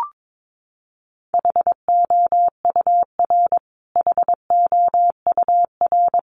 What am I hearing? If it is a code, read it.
Morse code, heard as a keyed tone, about 22 wpm, 700 Hz. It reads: HOUR HOUR
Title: Top 500 Words Repeated 22wpm